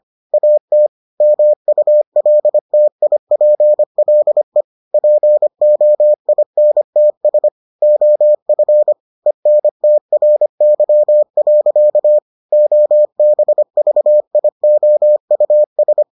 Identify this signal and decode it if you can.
Morse code, heard as a keyed tone, about 25 wpm, 600 Hz. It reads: AT MULTIPLE POINTS OF ENTRY. OBVIOUS